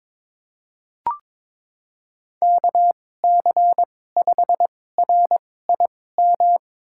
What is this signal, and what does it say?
Morse code, heard as a keyed tone, about 22 wpm, spaced as for 16 wpm, 700 Hz. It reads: KC5RIM
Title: USA Call Signs - 6 Characters - 2x Character Spacing - 22wpm